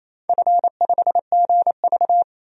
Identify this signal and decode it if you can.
Morse code, heard as a keyed tone, about 28 wpm, 700 Hz. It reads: F5GV
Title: DXCC-100 Call Signs - 4 Characters - 28wpm